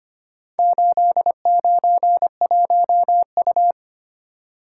Morse code, heard as a keyed tone, about 25 wpm, 700 Hz. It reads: 891U
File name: practice / 25wpm / Four Letter-Number Combinations - 25wpm